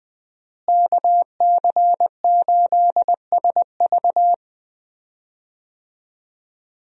Morse code, heard as a keyed tone, about 20 wpm, 700 Hz. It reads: KC8SV
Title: USA Call Signs - 5 Characters - Repeated 3x - 20wpm